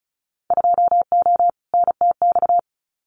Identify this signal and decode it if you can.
Morse code, heard as a keyed tone, about 34 wpm, 700 Hz. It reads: 2O NTX
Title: Winter Field Day Exchanges Encore 35wpm